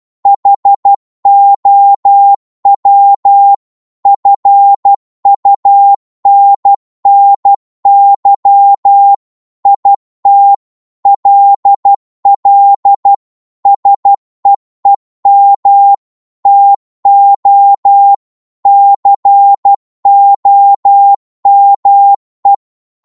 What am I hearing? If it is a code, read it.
Morse code, heard as a keyed tone, about 12 wpm, 800 Hz. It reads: HOW FUNNY IT LL SEEM TO COME
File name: Morse / psg-12wpm